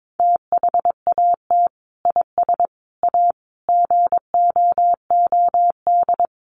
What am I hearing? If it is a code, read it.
Morse code, heard as a keyed tone, about 22 wpm, 700 Hz. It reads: THAT IS A GOOD
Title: Sentences from Top 700 Words 22wpm